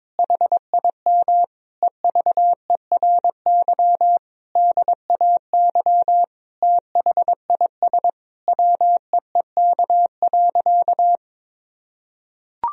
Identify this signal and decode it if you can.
Morse code, heard as a keyed tone, about 22 wpm, 700 Hz. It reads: HIM EVERY DAY THIS WEEK.
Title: Sentences from Top 1000 Words - Review Part 2 - 22wpm